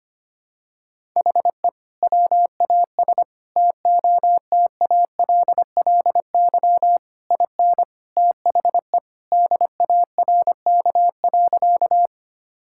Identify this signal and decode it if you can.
Morse code, heard as a keyed tone, about 25 wpm, 700 Hz. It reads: HE WAS TOTALLY IN THE DARK.
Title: Sentences from Top 1600 Words 25wpm